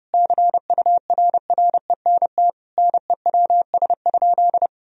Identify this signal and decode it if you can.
Morse code, heard as a keyed tone, about 32 wpm, 700 Hz. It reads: CURRENT NEWS?
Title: Sentences from Top 1000 Words - Review Part 4 - 30wpm